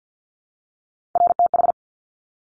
Morse code, heard as a keyed tone, about 50 wpm, 700 Hz. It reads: RTH
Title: Top Three Letter Combinations in Words 50wpm